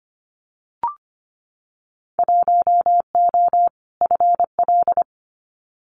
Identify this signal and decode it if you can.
Morse code, heard as a keyed tone, about 25 wpm, 700 Hz. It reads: 1O FL
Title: Winter Field Day Exchanges 25wpm